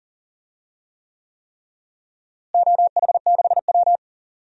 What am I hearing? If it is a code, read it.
Morse code, heard as a keyed tone, about 39 wpm, 700 Hz. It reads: OH6W
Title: DXCC-100 Call Signs - 4 Characters - Repeated 3x - 40wpm